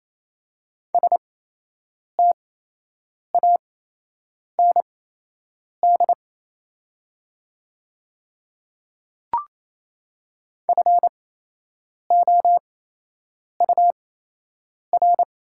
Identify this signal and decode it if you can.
Morse code, heard as a keyed tone, about 28 wpm, spaced as for 8 wpm, 700 Hz. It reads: STAND FOUR
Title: 8x Character Spacing - Top 300 Words - 28wpm